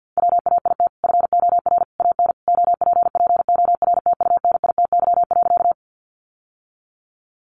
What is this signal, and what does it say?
Morse code, heard as a keyed tone, about 50 wpm, 700 Hz. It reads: WAIT FOR AN OPPORTUNITY.
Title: Sentences from Top 1000 Words 50wpm